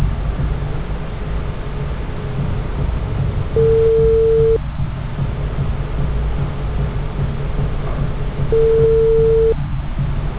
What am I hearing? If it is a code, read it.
Ringback tone (China, 450 Hz).
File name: out- -2009- - - .16998